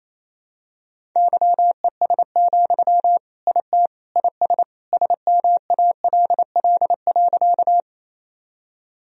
Morse code, heard as a keyed tone, about 28 wpm, 700 Hz. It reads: YES, IT IS SMALL.
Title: Sentences from Top 1000 Words - Review Part 1 - 28wpm